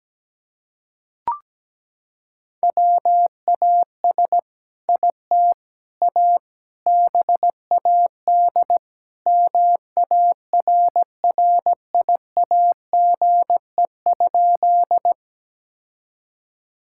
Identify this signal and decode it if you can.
Morse code, heard as a keyed tone, about 17 wpm, 700 Hz. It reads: WAS IT A BAD MARRIAGE?
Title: Sentences from Top 1000 Words - Review Part 5 - 17wpm